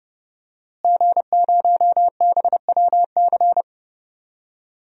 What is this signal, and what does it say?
Morse code, heard as a keyed tone, about 32 wpm, 700 Hz. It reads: G0BWC
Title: DXCC-100 Call Signs - 5 Characters - Repeated 3x - 30wpm